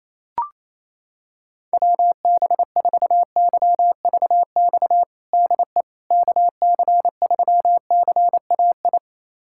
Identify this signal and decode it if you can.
Morse code, heard as a keyed tone, about 28 wpm, 700 Hz. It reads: WB4YVX DE KC3CAI
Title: USA Callsign DE Callsign - 6 Characters - 28wpm